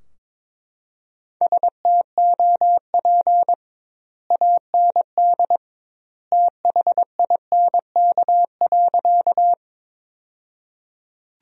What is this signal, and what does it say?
Morse code, heard as a keyed tone, about 21 wpm, 700 Hz. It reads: STOP AND THINK.
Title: 2x Word-Spacing - Sentences from Top 300 Words - 22wpm